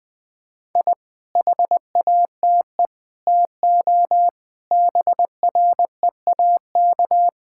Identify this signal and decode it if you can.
Morse code, heard as a keyed tone, about 20 wpm, 700 Hz. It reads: I HATE TO BREAK
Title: Sentences with 8-Word Phrases - 20wpm